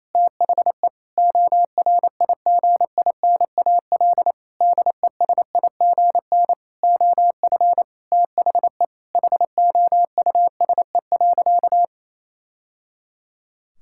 Morse code, heard as a keyed tone, about 28 wpm, 700 Hz. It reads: THE ORIGINAL DESIGN OF THE HOUSE.
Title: Sentences from Top 1000 Words - Review Part 6 - 28wpm